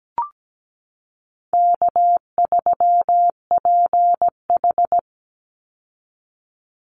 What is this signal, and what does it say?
Morse code, heard as a keyed tone, about 17 wpm, 700 Hz. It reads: K3PH
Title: Easy Call Signs - 17wpm